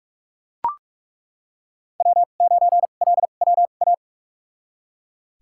Morse code, heard as a keyed tone, about 41 wpm, spaced as for 31 wpm, 700 Hz. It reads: W9LWA
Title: USA Call Signs - 2x Character Spacing - 45wpm